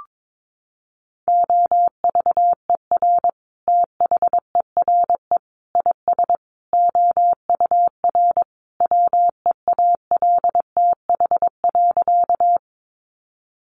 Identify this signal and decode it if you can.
Morse code, heard as a keyed tone, about 22 wpm, 700 Hz. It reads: OVER THERE IS OUR WEALTH.
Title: Sentences from Top 1200 Words 22wpm